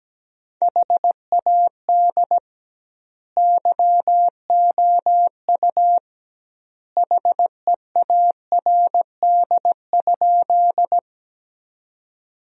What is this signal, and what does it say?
Morse code, heard as a keyed tone, about 17 wpm, 700 Hz. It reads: HAD YOU HEARD?
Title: 2x Word-Spacing - Sentences from Top 400 Words - 17wpm